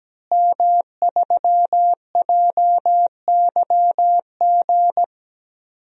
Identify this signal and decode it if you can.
Morse code, heard as a keyed tone, about 17 wpm, 700 Hz. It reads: M3JYG
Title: DXCC-100 Call Signs - 5 Characters - Repeated 3x - 17wpm